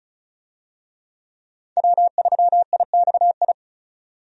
Morse code, heard as a keyed tone, about 36 wpm, 700 Hz. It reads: W3IXI
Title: USA Call Signs - 5 Characters - 35wpm